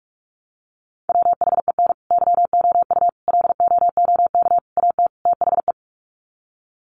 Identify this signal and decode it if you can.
Morse code, heard as a keyed tone, about 44 wpm, 700 Hz. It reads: WHEN YOU LOOK AT THE
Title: Sets of 5 Words 45wpm